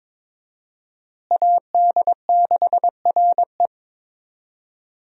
Morse code, heard as a keyed tone, about 22 wpm, 700 Hz. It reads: AD6RE
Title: USA Call Signs - 5 Characters - Repeated 3x - 22wpm